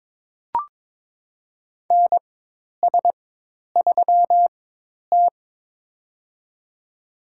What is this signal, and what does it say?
Morse code, heard as a keyed tone, about 22 wpm, spaced as for 10 wpm, 700 Hz. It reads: NS3T
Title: USA Call Signs - 4 Characters - 4x Character Spacing - 22wpm